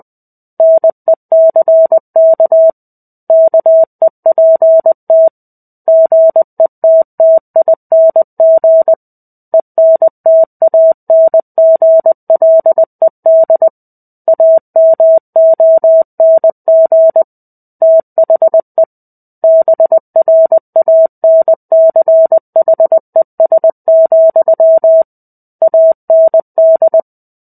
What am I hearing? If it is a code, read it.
Morse code, heard as a keyed tone, about 20 wpm, 650 Hz. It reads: NECK KEPT GETTING ENTANGLED AMONG THE BRANCHES, AND
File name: 05-Chapter-05 20 WPM